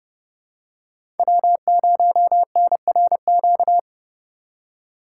Morse code, heard as a keyed tone, about 29 wpm, 700 Hz. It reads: W0NRQ
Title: USA Call Signs - 5 Characters - 3x Character Spacing - 30wpm